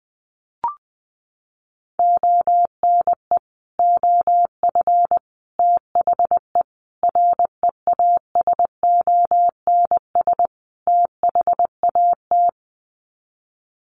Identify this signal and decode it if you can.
Morse code, heard as a keyed tone, about 20 wpm, 700 Hz. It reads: ONE OF THE REASONS THAT
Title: Sets of 5 Words Encore 20wpm